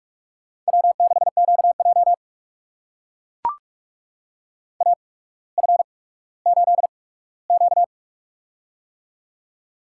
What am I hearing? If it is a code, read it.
Morse code, heard as a keyed tone, about 44 wpm, spaced as for 19 wpm, 700 Hz. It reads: W6QJ AF8Q
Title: USA Call Signs - 4 Characters - 8x Character Spacing - 45wpm